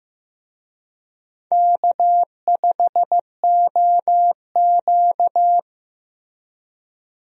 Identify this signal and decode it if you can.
Morse code, heard as a keyed tone, about 15 wpm, 700 Hz. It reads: K5OQ
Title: USA Call Signs - 4 Characters - Repeated 2x - 15wpm